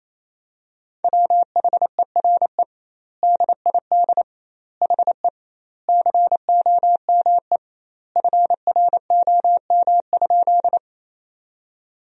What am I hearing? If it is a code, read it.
Morse code, heard as a keyed tone, about 27 wpm, 700 Hz. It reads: WHERE DID HE COME FROM?
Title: 2x Word-Spacing - Sentences from Top 200 Words - 28wpm